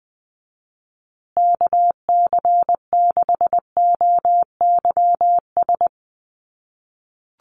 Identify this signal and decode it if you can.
Morse code, heard as a keyed tone, about 20 wpm, 700 Hz. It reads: KC6OYS
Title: USA Call Signs Repeated 2x 20wpm